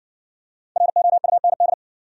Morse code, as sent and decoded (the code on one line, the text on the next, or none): .-. --- ..- -. -..
ROUND